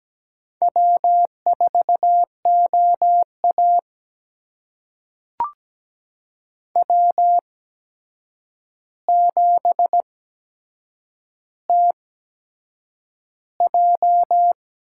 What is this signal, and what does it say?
Morse code, heard as a keyed tone, about 17 wpm, spaced as for 7 wpm, 700 Hz. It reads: W4OA W7TJ
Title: USA Call Signs - 4 Characters - 8x Character Spacing - 17wpm